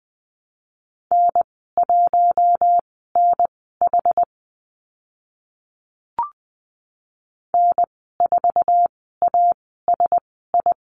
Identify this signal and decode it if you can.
Morse code, heard as a keyed tone, about 20 wpm, spaced as for 14 wpm, 700 Hz. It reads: N1NH N4ASI
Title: 082 Lesson 14g - Number 2 - Call Signs 20wpm